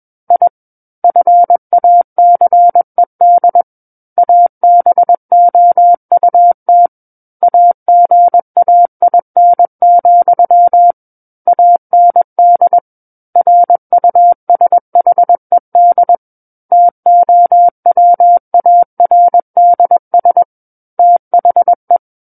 Morse code, heard as a keyed tone, about 21 wpm, 700 Hz. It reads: I FACED ABOUT AGAIN, AND RUSHED TOWARDS THE
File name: Book One / 12-Chapter 12 - 21 WPM